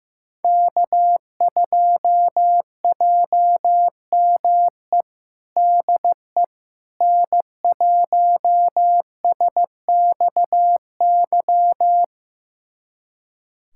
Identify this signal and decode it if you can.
Morse code, heard as a keyed tone, about 16 wpm, 700 Hz. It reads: K2JME DE N1SXY
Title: USA Callsign DE Callsign - 5 Characters - 15wpm